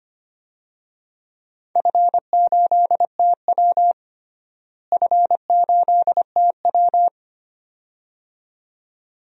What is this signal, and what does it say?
Morse code, heard as a keyed tone, about 25 wpm, 700 Hz. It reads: F8TW F8TW
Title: DXCC-100 Call Signs - 4 Characters - Repeated 2x - 25wpm